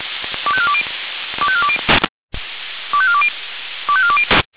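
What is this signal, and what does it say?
Five-tone selective call (ZVEI), heard as quick runs of five tones, tones of about 70 ms. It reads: 25420 25420 25420 25420
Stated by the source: ZVEI five-tone selective call 25431 received on an SDR.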